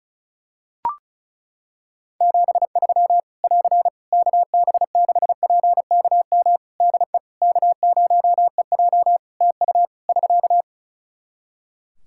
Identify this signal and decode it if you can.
Morse code, heard as a keyed tone, about 36 wpm, 700 Hz. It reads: 73 <AR> KB6PKM DE K0EJ TU <SK>
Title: Practice QSO - Exchange 4 - Ending - 35wpm